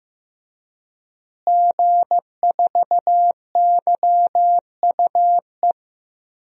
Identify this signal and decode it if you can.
Morse code, heard as a keyed tone, about 15 wpm, 700 Hz. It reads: G4YUE
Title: DXCC-100 Call Signs - 5 Characters - Repeated 2x - 15wpm